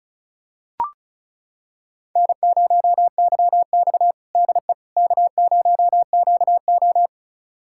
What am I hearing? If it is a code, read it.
Morse code, heard as a keyed tone, about 36 wpm, 700 Hz. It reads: N0YX DE K0QO
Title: USA Callsign DE Callsign - 4 Characters - 35wpm